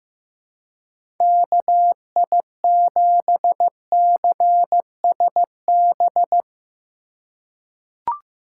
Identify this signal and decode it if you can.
Morse code, heard as a keyed tone, about 15 wpm, 700 Hz. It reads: KI7CSB
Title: USA Call Signs - 6 Characters - 4x Character Spacing - 15wpm